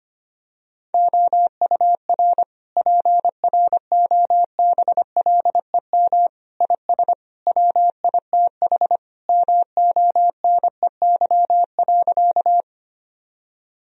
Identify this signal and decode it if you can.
Morse code, heard as a keyed tone, about 25 wpm, 700 Hz. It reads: OUR PROBLEM IS WITH MONEY.
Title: Sentences from Top 1000 Words - Review Part 2 - 25wpm